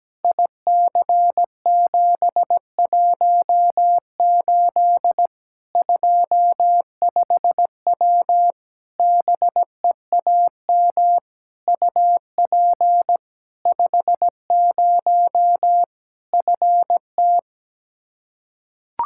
Morse code, heard as a keyed tone, about 17 wpm, 700 Hz. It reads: IC718 25W BEAM UP 50 FT
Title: K1USN SST QSO Practice 17wpm